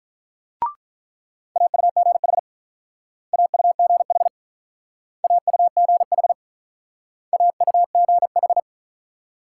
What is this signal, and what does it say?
Morse code, heard as a keyed tone, about 43 wpm, 700 Hz. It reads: AUGH AUGH AUGH AUGH
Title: Speed-Racing Multi-letter Phonemes - 35wpm